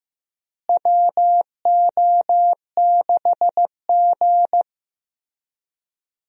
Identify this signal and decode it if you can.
Morse code, heard as a keyed tone, about 15 wpm, 700 Hz. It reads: WO6G
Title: USA Call Signs - 4 Characters - Repeated 3x - 15wpm